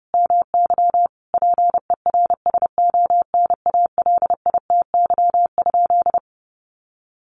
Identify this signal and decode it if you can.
Morse code, heard as a keyed tone, about 30 wpm, 700 Hz. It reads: MY PERSONALITY?
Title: Sentences from Top 1400 Words 30wpm